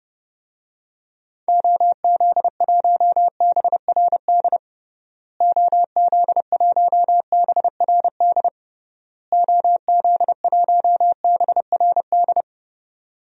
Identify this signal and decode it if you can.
Morse code, heard as a keyed tone, about 30 wpm, 700 Hz. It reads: OZ1BRD OZ1BRD OZ1BRD
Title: DXCC-100 Call Signs - 6 Characters - Repeated 3x - 30wpm